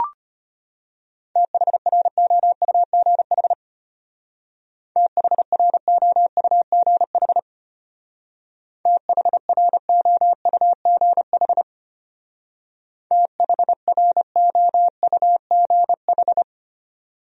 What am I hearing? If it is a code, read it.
Morse code, heard as a keyed tone, about 32 wpm, 700 Hz. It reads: THROUGH THROUGH THROUGH THROUGH